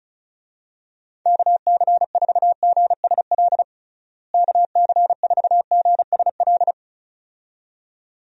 Morse code, heard as a keyed tone, about 34 wpm, 700 Hz. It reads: KC4GSL KC4GSL
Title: USA Call Signs - 6 Characters - Repeated 2x - 35wpm